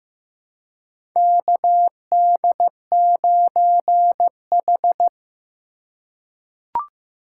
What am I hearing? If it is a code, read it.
Morse code, heard as a keyed tone, about 15 wpm, 700 Hz. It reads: KD9H
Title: USA Call Signs - 4 Characters - 4x Character Spacing - 15wpm